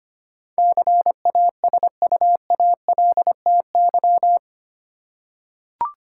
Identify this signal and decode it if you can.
Morse code, heard as a keyed tone, about 25 wpm, 700 Hz. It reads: CASUALTY
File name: 8 Letter Words - First 3 Letters Given - 25wpm